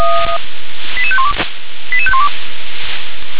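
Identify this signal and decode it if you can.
Five-tone selective call (ZVEI), heard as quick runs of five tones, tones of about 70 ms. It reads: 88512 88512
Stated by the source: ZVEI five-tone selective call 88730 received on SDR.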